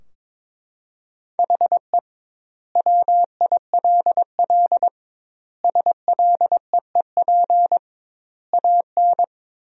Morse code, heard as a keyed tone, about 22 wpm, 700 Hz. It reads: HE WILL SLEEP AN
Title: 2x Word-Spacing - Sentences from Top 500 Words - 22wpm